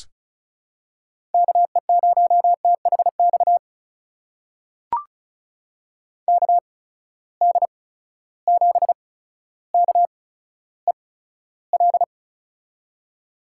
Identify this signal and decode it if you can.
Morse code, heard as a keyed tone, about 36 wpm, spaced as for 15 wpm, 700 Hz. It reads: KE0THX KD7KEL
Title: USA Call Signs - 6 Characters - 8x Character Spacing - 35wpm